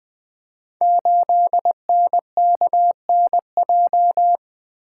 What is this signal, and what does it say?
Morse code, heard as a keyed tone, about 20 wpm, 700 Hz. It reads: 8NKNJ
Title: Code Groups - 5 Letter:Numbers - 20wpm